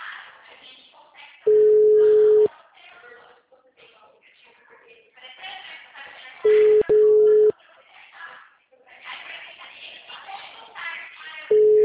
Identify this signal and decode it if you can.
Ringback tone (Europe, 425 Hz), with speech.
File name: out- -2001- - - .25098